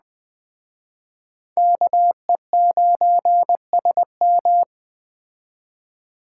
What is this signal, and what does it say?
Morse code, heard as a keyed tone, about 20 wpm, 700 Hz. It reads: KE9SM